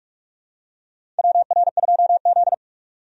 Morse code, heard as a keyed tone, about 42 wpm, 700 Hz. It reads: WR2Z
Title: USA Call Signs - 4 Characters - 45wpm